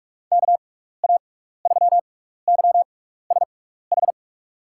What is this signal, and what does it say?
Morse code, heard as a keyed tone, about 46 wpm, spaced as for 15 wpm, 700 Hz. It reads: KA3YSH
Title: USA Call Signs - 6x Character Spacing - 45wpm